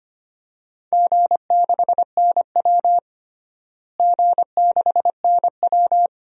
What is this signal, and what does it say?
Morse code, heard as a keyed tone, about 25 wpm, 700 Hz. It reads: G6NW G6NW
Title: DXCC-100 Call Signs - 4 Characters - Repeated 2x - 25wpm